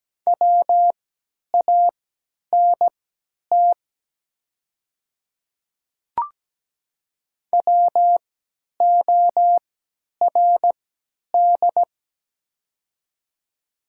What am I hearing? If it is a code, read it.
Morse code, heard as a keyed tone, about 17 wpm, spaced as for 10 wpm, 700 Hz. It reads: WANT WORD
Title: 3x Character Spacing - Top 200 Words - 17wpm